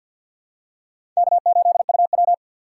Morse code, heard as a keyed tone, about 52 wpm, 700 Hz. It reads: K8UW
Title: USA Call Signs - 4 Characters - 3x Character Spacing - 50wpm